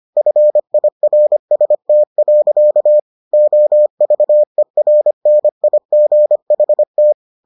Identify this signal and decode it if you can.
Morse code, heard as a keyed tone, about 25 wpm, 600 Hz. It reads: FIRST. OVERNIGHT